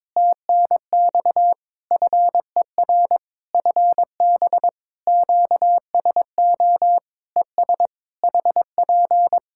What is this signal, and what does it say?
Morse code, heard as a keyed tone, about 22 wpm, 700 Hz. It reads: TNX FER FB QSO ES HP
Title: K1USN SST QSO Practice 22wpm